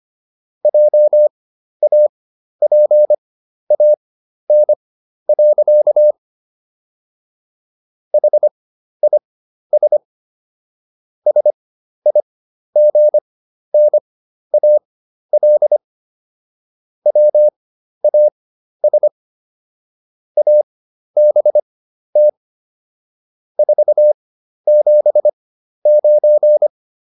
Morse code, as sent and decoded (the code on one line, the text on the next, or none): .--- .- .--. .- -. .-.-.- / .... .. ... / ... .. --. -. .- .-.. / .-- .- ... / .- -... - / ....- --... ----.
JAPAN. HIS SIGNAL WAS ABT 479